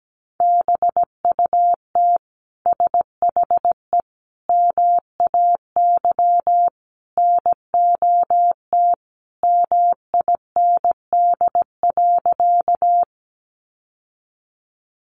Morse code, heard as a keyed tone, about 17 wpm, 700 Hz. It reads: BUT SHE MAY NOT MIND.
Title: Sentences from Top 600 Words 17wpm